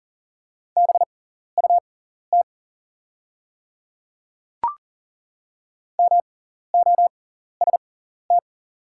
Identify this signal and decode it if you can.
Morse code, heard as a keyed tone, about 38 wpm, spaced as for 14 wpm, 690 Hz. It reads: BUT MOST
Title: 6x Character Spacing - Top 100 Words - 40wpm